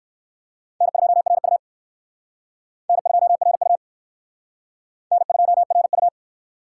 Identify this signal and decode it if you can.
Morse code, heard as a keyed tone, about 55 wpm, 700 Hz. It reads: N2RU N2RU N2RU
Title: Speed Racing - USA 1x2 Call Signs 45wpm